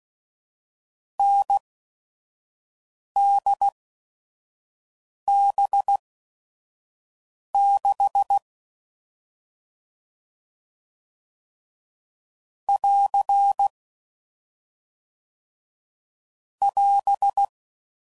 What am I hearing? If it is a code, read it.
Morse code, heard as a keyed tone, about 16 wpm, spaced as for 5 wpm, 800 Hz. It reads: NDB6 <AR> <AS>
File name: Combinations / Difficult Combos / NDB NDB6